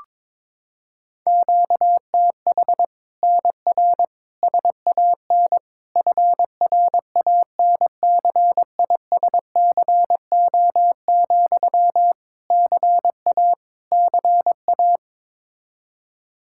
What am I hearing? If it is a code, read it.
Morse code, heard as a keyed tone, about 22 wpm, 700 Hz. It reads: QTH NR SAN FRANCISCO, CA CA
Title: Practice QSO - Exchange 1 - Beginning - 22wpm